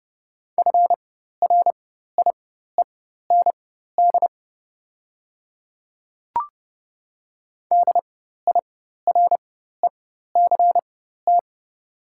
Morse code, heard as a keyed tone, about 30 wpm, spaced as for 14 wpm, 700 Hz. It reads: FRIEND DIRECT
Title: 4x Character Spacing - Top 500 Words - 30wpm